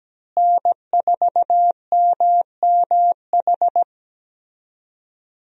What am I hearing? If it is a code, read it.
Morse code, heard as a keyed tone, about 17 wpm, 700 Hz. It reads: N4MMH
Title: USA Call Signs - 5 Characters - Repeated 3x - 17wpm